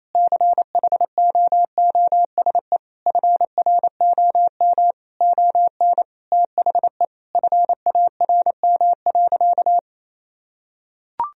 Morse code, heard as a keyed tone, about 28 wpm, 700 Hz. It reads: CHOOSE FROM ON THE FARM.
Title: Sentences from Top 1000 Words - Review Part 4 - 28wpm